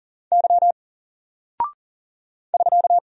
Morse code, heard as a keyed tone, about 40 wpm, 700 Hz. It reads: Y <SK>
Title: Single Character (Letter:Number:Punctuation:Prosign) Alt 40wpm